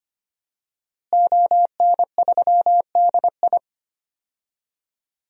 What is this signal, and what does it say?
Morse code, heard as a keyed tone, about 25 wpm, 700 Hz. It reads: ON3DI
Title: DXCC-100 Call Signs - 5 Characters - Repeated 3x - 25wpm